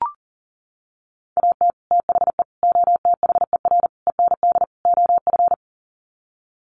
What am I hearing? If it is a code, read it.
Morse code, heard as a keyed tone, about 40 wpm, 700 Hz. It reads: AT THE OTHER END OF
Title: Sets of 5 Words 40wpm